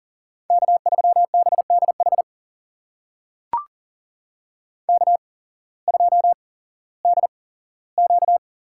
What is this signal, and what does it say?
Morse code, heard as a keyed tone, about 40 wpm, spaced as for 19 wpm, 700 Hz. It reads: K3BDH K2DQ
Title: USA Call Signs - 8x Character Spacing - 40wpm